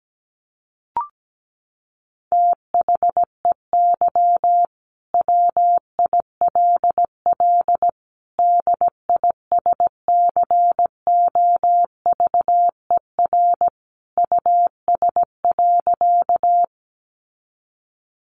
Morse code, heard as a keyed tone, about 17 wpm, 700 Hz. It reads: THEY WILL DISCOVER US.
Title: Sentences from Top 1100 Words 17wpm